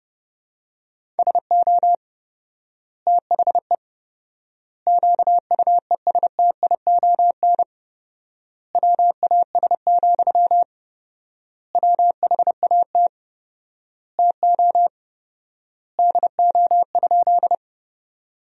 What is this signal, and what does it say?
Morse code, heard as a keyed tone, about 30 wpm, 700 Hz. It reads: SO THE QUESTION WAS, WHAT TO DO?
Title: 4x Word-Spacing - Sentences from Top 400 Words - 30wpm